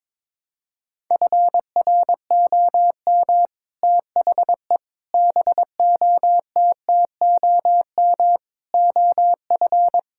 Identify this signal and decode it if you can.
Morse code, heard as a keyed tone, about 22 wpm, 700 Hz. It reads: FROM THE BOTTOM OF
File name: practice / 22wpm / Sets of 6 Words - 22wpm